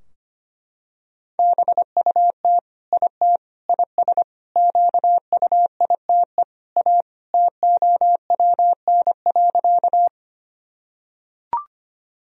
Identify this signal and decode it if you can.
Morse code, heard as a keyed tone, about 25 wpm, 700 Hz. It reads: BUT IT IS QUITE A TOWN.
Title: Sentences from Top 1000 Words - Review Part 2 - 25wpm